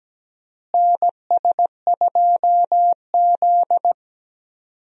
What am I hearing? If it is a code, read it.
Morse code, heard as a keyed tone, about 17 wpm, 700 Hz. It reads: NS2Z
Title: USA Call Signs - 4 Characters - Repeated 2x - 17wpm